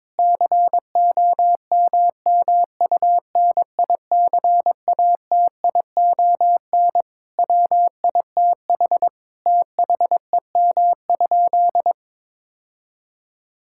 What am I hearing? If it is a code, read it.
Morse code, heard as a keyed tone, about 22 wpm, 700 Hz. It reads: COMMUNICATION WITH THEM?
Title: Sentences from Top 1000 Words - Review Part 6 - 22wpm